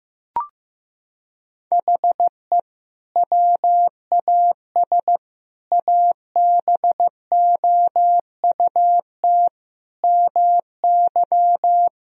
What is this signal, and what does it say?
Morse code, heard as a keyed tone, about 16 wpm, 700 Hz. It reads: HE WAS ABOUT MY
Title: Sentences from Top 1000 Words - Review Part 2 - 15wpm